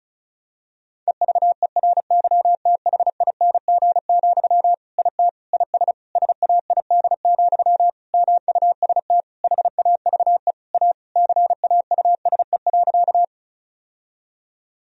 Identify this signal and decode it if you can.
Morse code, heard as a keyed tone, about 36 wpm, 700 Hz. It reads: EVERYTHING, IT IS SAID, MUST HAVE A CAUSE.
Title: Sentences from Top 1000 Words - Review Part 1 - 35wpm